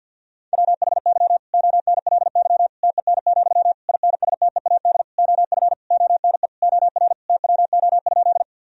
Morse code, heard as a keyed tone, about 50 wpm, 700 Hz. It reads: WHY ONLY TEN, INSTEAD OF ONE OR TWO?
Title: Sentences from Top 1000 Words - Review Part 4 - 50wpm